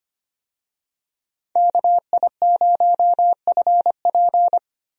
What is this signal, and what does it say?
Morse code, heard as a keyed tone, about 25 wpm, 700 Hz. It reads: KI0FP
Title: USA Call Signs - 5 Characters - 25wpm